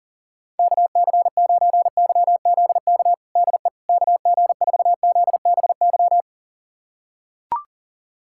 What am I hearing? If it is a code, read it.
Morse code, heard as a keyed tone, about 37 wpm, 700 Hz. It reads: KC9YZK DE KG4ZBY
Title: USA Callsign DE Callsign 40wpm